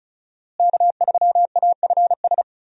Morse code, heard as a keyed tone, about 34 wpm, 700 Hz. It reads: K3AFS
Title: USA Call Signs - 5 Characters - Repeated 2x - 35wpm